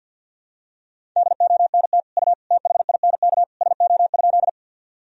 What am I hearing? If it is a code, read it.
Morse code, heard as a keyed tone, about 52 wpm, 700 Hz. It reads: DONT U THINK SO?
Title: Sentences from Top 300 Words and Abbreviations 50wpm